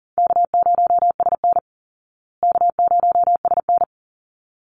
Morse code, heard as a keyed tone, about 40 wpm, 700 Hz. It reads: K0SN K0SN
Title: Easy Call Signs Repeated 2x - 40wpm